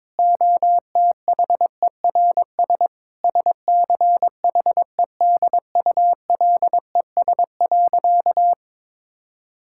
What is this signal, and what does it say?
Morse code, heard as a keyed tone, about 22 wpm, 700 Hz. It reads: OTHERS SCHEDULES.
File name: Difficult Sentences with Hints - 22wpm